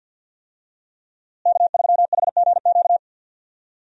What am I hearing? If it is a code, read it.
Morse code, heard as a keyed tone, about 52 wpm, 700 Hz. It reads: K3HGQ
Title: USA Call Signs - 5 Characters - Repeated 3x - 50wpm